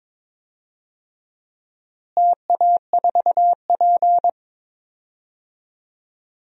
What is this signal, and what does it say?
Morse code, heard as a keyed tone, about 22 wpm, 700 Hz. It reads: TA4P
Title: DXCC-100 Call Signs - 4 Characters - Repeated 3x - 22wpm